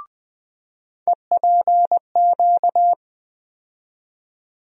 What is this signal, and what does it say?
Morse code, heard as a keyed tone, about 20 wpm, 700 Hz. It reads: EPQ
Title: USA Call Sign Postfixes - 3 Letters 20wpm